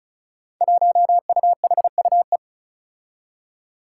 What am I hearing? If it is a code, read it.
Morse code, heard as a keyed tone, about 36 wpm, 700 Hz. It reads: 1UHUE
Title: Five Challenging Letter-Number Combinations 35wpm